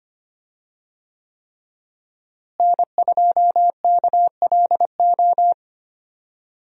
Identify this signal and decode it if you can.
Morse code, heard as a keyed tone, about 25 wpm, 700 Hz. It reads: N2KLO
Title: USA Call Signs - 5 Characters - Repeated 3x - 25wpm